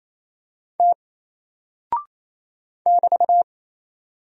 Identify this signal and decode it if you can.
Morse code, heard as a keyed tone, about 27 wpm, 700 Hz. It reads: T <BT>
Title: Single Character (Letter:Number:Punctuation:Prosign) 28wpm